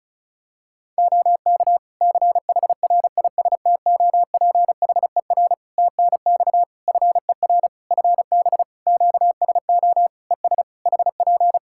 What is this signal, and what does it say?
Morse code, heard as a keyed tone, about 34 wpm, 700 Hz. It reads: OK CHRISTOPHER TNX FER FB QSO ES HP
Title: Practice QSO - Exchange 4 - Ending - 35wpm